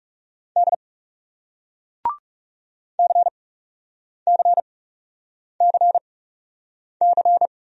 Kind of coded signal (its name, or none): Morse code